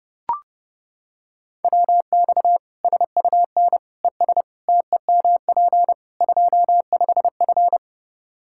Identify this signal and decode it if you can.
Morse code, heard as a keyed tone, about 30 wpm, 700 Hz. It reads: WX SUN ES TEMP 25F
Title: Practice QSO - Exchange 2 - Info - 30wpm